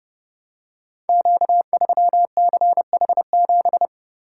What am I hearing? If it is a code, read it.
Morse code, heard as a keyed tone, about 30 wpm, 700 Hz. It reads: Q3CH7
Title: Code Groups - 5 Letter:Numbers - 30wpm